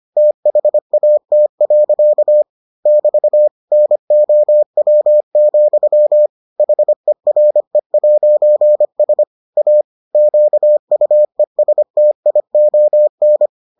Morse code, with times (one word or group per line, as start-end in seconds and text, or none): 0.2-2.4 THAT.
2.9-3.5 <BT>
3.7-6.3 NOW,
6.6-9.2 HERE'S
9.6-9.8 A
10.2-13.5 QUESTION